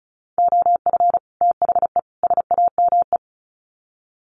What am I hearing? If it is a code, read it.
Morse code, heard as a keyed tone, about 34 wpm, 700 Hz. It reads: OF THE SAME
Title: Sets of 3 Words 35wpm